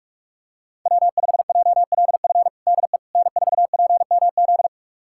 Morse code, heard as a keyed tone, about 46 wpm, 700 Hz. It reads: W5JLF DE N4PMZ
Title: USA Callsign DE Callsign - 5 Characters - 45wpm